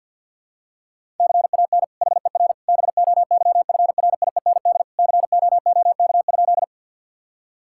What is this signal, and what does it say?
Morse code, heard as a keyed tone, about 50 wpm, 700 Hz. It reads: CAN HER BOYFRIEND COOK?
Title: Sentences from Top 1000 Words - Review Part 5 - 50wpm